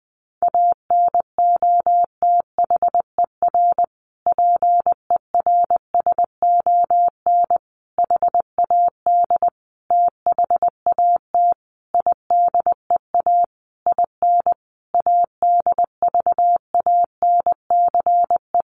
Morse code, heard as a keyed tone, about 19 wpm, 700 Hz. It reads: ANOTHER PERSON HAD THAT IDEA IN ADVANCE
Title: Sentences from Top 1400 Words 20wpm